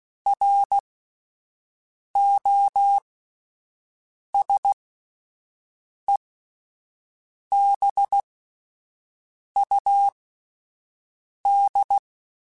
Morse code, heard as a keyed tone, about 16 wpm, spaced as for 5 wpm, 800 Hz. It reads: ROSEBUD